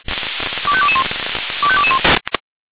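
Five-tone selective call (ZVEI), heard as quick runs of five tones, tones of about 70 ms. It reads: 25221 25221
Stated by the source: ZVEI five-tone selective call received on an SDR.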